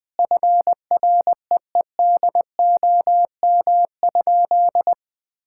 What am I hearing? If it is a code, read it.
Morse code, heard as a keyed tone, about 21 wpm, 700 Hz. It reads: FREEDOM?
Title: Sentences from Top 1000 Words - Review Part 5 - 20wpm